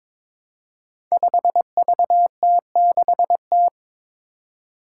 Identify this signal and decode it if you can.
Morse code, heard as a keyed tone, about 22 wpm, 700 Hz. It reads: 5VT6T
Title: Five Challenging Letter-Number Combinations 22wpm